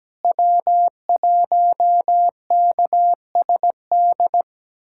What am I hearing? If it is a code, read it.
Morse code, heard as a keyed tone, about 17 wpm, 700 Hz. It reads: W1KSD
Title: USA Callsign DE Callsign 17wpm